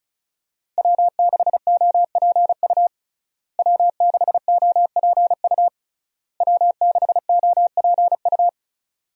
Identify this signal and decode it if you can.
Morse code, heard as a keyed tone, about 36 wpm, 700 Hz. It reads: W6OPU W6OPU W6OPU
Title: USA Call Signs - 5 Characters - Repeated 3x - 35wpm